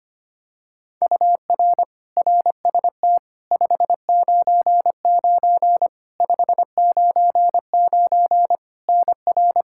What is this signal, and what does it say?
Morse code, heard as a keyed tone, about 25 wpm, 700 Hz. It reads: UR RST 599 599 NR
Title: Practice QSO - Exchange 1 - Beginning - 25wpm